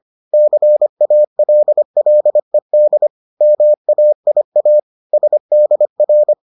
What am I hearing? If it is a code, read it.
Morse code, heard as a keyed tone, about 25 wpm, 600 Hz. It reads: CALLED MAIA SDR